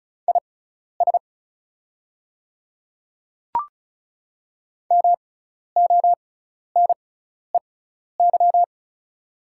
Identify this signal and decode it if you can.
Morse code, heard as a keyed tone, about 36 wpm, spaced as for 12 wpm, 700 Hz. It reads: IS MONEY
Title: 6x Character Spacing - Top 500 Words - 35wpm